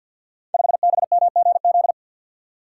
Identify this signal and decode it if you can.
Morse code, heard as a keyed tone, about 52 wpm, 700 Hz. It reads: 5BMGZ
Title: Code Groups - 5 Letter:Numbers - 50wpm